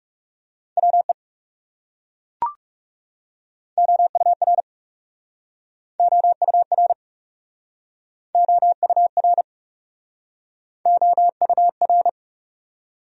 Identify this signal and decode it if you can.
Morse code, heard as a keyed tone, about 37 wpm, 700 Hz. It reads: WE OUR OUR OUR OUR
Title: Speed-Racing Top 200 Words 30wpm